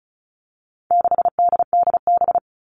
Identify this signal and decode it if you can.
Morse code, heard as a keyed tone, about 34 wpm, 700 Hz. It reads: 6DDB
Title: All Cut Numbers - 4-Digit 35wpm